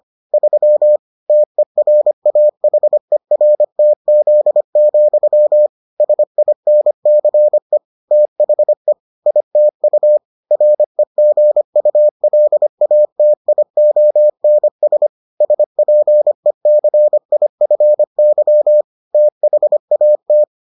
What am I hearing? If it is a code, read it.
Morse code, heard as a keyed tone, about 25 wpm, 600 Hz. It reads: 3 TERAHERTZ, SINCE THE ITU REGULATIONS SPECIFY THAT